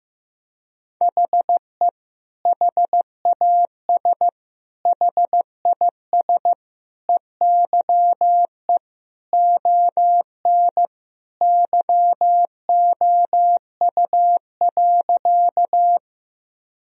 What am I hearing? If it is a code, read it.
Morse code, heard as a keyed tone, about 15 wpm, 700 Hz. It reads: HE HAS HIS EYE ON YOU.
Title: Sentences from Top 1000 Words - Review Part 1 - 15wpm